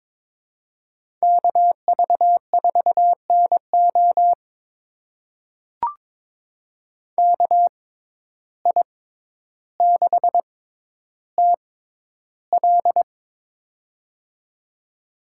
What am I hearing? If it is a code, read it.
Morse code, heard as a keyed tone, about 22 wpm, spaced as for 11 wpm, 700 Hz. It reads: KV4NO KI6TL
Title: USA Call Signs - 5 Characters - 6x Character Spacing - 22wpm